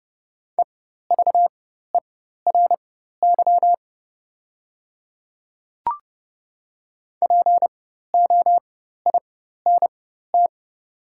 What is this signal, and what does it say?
Morse code, heard as a keyed tone, about 30 wpm, spaced as for 14 wpm, 690 Hz. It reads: EVERY POINT
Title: 4x Character Spacing - Top 500 Words - 30wpm